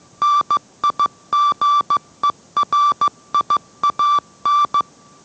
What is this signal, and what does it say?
Morse code, heard as a keyed tone, about 15 wpm, 1170 Hz. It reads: NIGERIAN